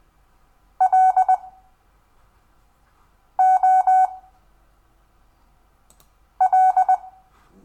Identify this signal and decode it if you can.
Morse code, heard as a keyed tone, about 20 wpm, 750 Hz. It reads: L O L
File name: cw / Lesson5 LO